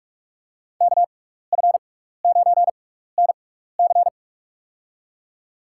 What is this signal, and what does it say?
Morse code, heard as a keyed tone, about 47 wpm, spaced as for 16 wpm, 700 Hz. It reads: KF9NC
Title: USA Call Signs - 5 Characters - 6x Character Spacing - 45wpm